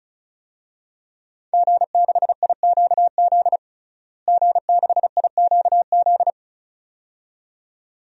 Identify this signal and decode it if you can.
Morse code, heard as a keyed tone, about 36 wpm, 700 Hz. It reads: G6IQZ G6IQZ
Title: DXCC-100 Call Signs - 5 Characters - Repeated 2x - 35wpm